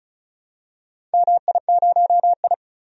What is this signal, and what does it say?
Morse code, heard as a keyed tone, about 36 wpm, 700 Hz. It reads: MI0I
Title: DXCC-100 Call Signs - 4 Characters - 35wpm